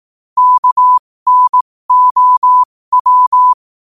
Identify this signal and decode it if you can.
Morse code, heard as a keyed tone, about 18 wpm, spaced as for 15 wpm, 1000 Hz. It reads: KNOW
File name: qotdfiles / 15wpm